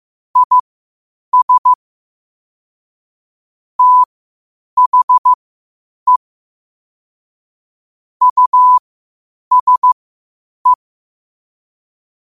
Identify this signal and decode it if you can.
Morse code, heard as a keyed tone, about 15 wpm, spaced as for 8 wpm, 1000 Hz. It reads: IS THE USE
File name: qotdfiles / 8wpm